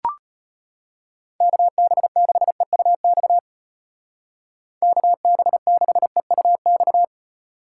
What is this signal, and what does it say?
Morse code, heard as a keyed tone, about 34 wpm, 700 Hz. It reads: KB6EUX KB6EUX